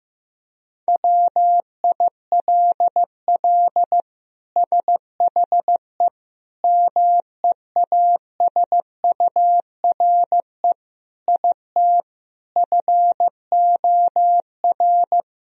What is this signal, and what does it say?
Morse code, heard as a keyed tone, about 15 wpm, 700 Hz. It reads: WILL SHE MEASURE IT FOR
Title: Sentences from Top 1000 Words - Review Part 2 - 15wpm